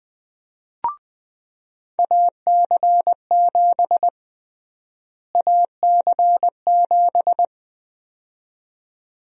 Morse code, heard as a keyed tone, about 21 wpm, 700 Hz. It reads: AC7 AC7
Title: USA Call Sign Prefixes - Repeated 2x - 20wpm